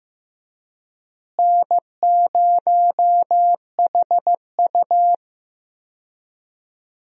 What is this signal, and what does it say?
Morse code, heard as a keyed tone, about 15 wpm, 700 Hz. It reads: N0HU
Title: USA Call Signs - 4 Characters - Repeated 3x - 15wpm